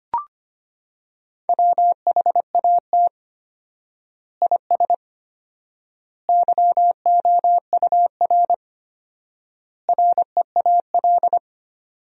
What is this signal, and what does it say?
Morse code, heard as a keyed tone, about 25 wpm, 700 Hz. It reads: WHAT IS YOUR REAL
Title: 4x Word-Spacing - Sentences from Top 300 Words - 25wpm